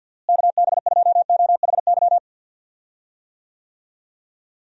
Morse code, heard as a keyed tone, about 52 wpm, 700 Hz. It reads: KB2OHY
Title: USA Call Signs - 6 Characters - Repeated 3x - 50wpm